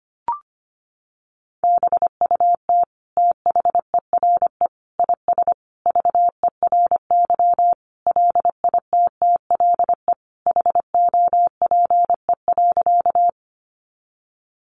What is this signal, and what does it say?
Morse code, heard as a keyed tone, about 25 wpm, 700 Hz. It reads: BUT THERE IS VERY LITTLE HOPE.
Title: Sentences from Top 1000 Words 25wpm